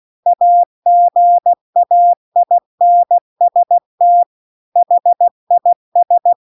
Morse code, heard as a keyed tone, about 16 wpm, 700 Hz. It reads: AGAINST HIS